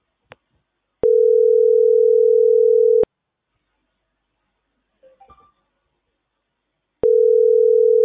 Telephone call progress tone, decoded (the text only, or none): ringback tone